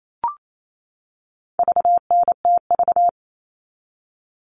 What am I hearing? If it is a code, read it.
Morse code, heard as a keyed tone, about 28 wpm, 700 Hz. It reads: VNTV
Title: All Cut Numbers - 4-Digit 28wpm